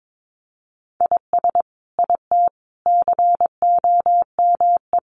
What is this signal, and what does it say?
Morse code, heard as a keyed tone, about 22 wpm, 700 Hz. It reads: IS IT COME
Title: Sentences from Top 500 Words 22wpm